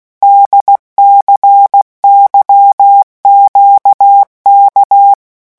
Morse code, heard as a keyed tone, about 16 wpm, 800 Hz. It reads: DCYQK
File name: morse mp3 / revision3-vitesse 15 mots